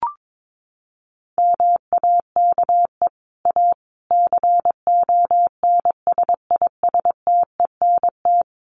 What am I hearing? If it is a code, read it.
Morse code, heard as a keyed tone, about 22 wpm, 700 Hz. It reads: MAKE A CONSISTENT
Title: Sentences from Top 1500 Words 22wpm